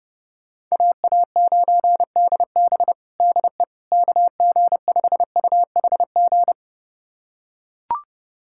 Morse code, heard as a keyed tone, about 30 wpm, 690 Hz. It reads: AA9DB DE KG5UHG
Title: USA Callsign DE Callsign 30wpm